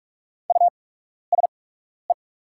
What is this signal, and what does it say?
Morse code, heard as a keyed tone, about 44 wpm, spaced as for 12 wpm, 690 Hz. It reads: USE
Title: 8x Character Spacing - Top 200 Words - 45wpm